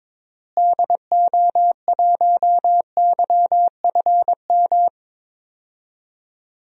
Morse code, heard as a keyed tone, about 22 wpm, 700 Hz. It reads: DO1YFM
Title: DXCC-100 Call Signs - 6 Characters - Repeated 2x - 22wpm